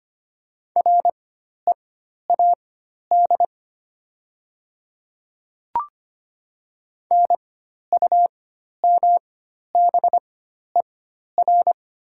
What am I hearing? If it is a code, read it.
Morse code, heard as a keyed tone, about 25 wpm, spaced as for 12 wpm, 690 Hz. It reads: READ NUMBER
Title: 4x Character Spacing - Top 300 Words - 25wpm